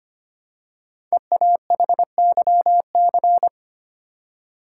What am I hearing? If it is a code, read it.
Morse code, heard as a keyed tone, about 25 wpm, 700 Hz. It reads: EAHYC
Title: Code Groups - 5 Letters - 25wpm